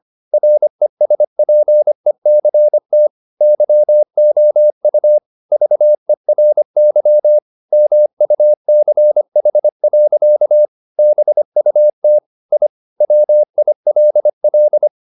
Morse code, heard as a keyed tone, about 25 wpm, 600 Hz. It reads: RESPECT YOU VERY MUCH. BUT I WILL
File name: the old man and the sea part 1 25wpm